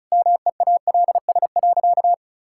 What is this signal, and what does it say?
Morse code, heard as a keyed tone, about 34 wpm, 700 Hz. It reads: MEALS.
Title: Pangram Challenge v2 - 35wpm